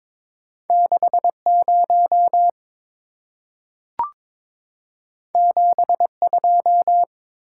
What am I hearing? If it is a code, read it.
Morse code, heard as a keyed tone, about 22 wpm, 700 Hz. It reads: 60 72
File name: practice / 22wpm / Two Letter-Number Combinations - 22wpm